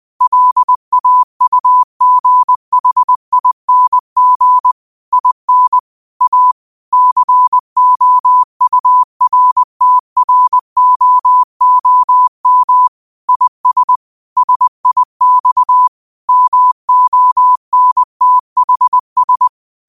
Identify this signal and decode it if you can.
Morse code, heard as a keyed tone, about 20 wpm, 1000 Hz. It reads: LAUGHING IN A COURTROOM IS SIX MONTHS